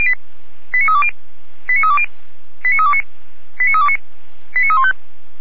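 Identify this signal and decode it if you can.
Five-tone selective call (ZVEI), heard as quick runs of five tones, tones of about 70 ms. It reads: 89320 89320 89329 89329 89315
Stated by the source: ZVEI five-tone call 89504 received on SDR.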